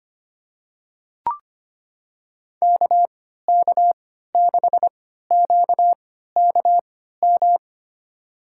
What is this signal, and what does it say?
Morse code, heard as a keyed tone, about 25 wpm, spaced as for 14 wpm, 700 Hz. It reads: KK6QKM
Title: USA Call Signs - 6 Characters - 3x Character Spacing - 25wpm